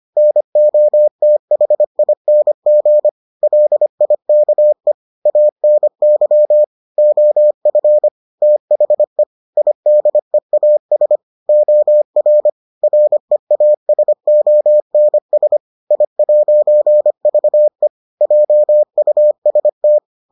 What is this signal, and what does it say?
Morse code, heard as a keyed tone, about 25 wpm, 600 Hz. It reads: NOTHING LIKE ANY OF THE IDEAS OR REASONS I'VE JUST